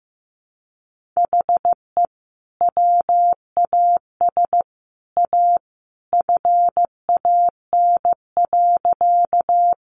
Morse code, heard as a keyed tone, about 15 wpm, 700 Hz. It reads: HE WAS A FAN.
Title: Sentences from Top 1300 Words 15wpm